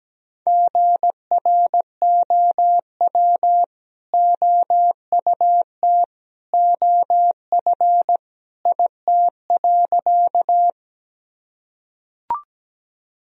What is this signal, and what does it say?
Morse code, heard as a keyed tone, about 17 wpm, 700 Hz. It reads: GROW OUT OF IT.
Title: Sentences from Top 1000 Words - Review Part 1 - 17wpm